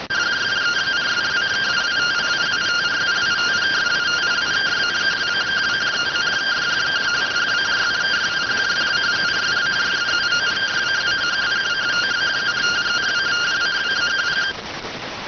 Two-tone teleprinter signal, about 170 Hz shift.